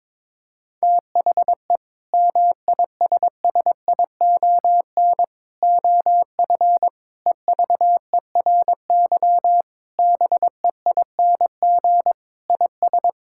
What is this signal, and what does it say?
Morse code, heard as a keyed tone, about 22 wpm, 700 Hz. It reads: THE MISSION OF EVERY BEING IS
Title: Sentences from Top 1600 Words 22wpm